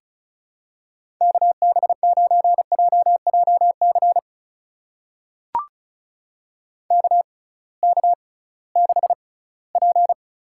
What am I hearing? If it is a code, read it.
Morse code, heard as a keyed tone, about 34 wpm, spaced as for 20 wpm, 700 Hz. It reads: KB9JJC KK6P
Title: USA Call Signs - 6 Characters - 6x Character Spacing - 35wpm